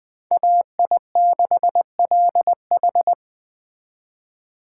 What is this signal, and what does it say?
Morse code, heard as a keyed tone, about 20 wpm, 700 Hz. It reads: AI6LH
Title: Speed Racing - USA 2x2 Call Signs 17wpm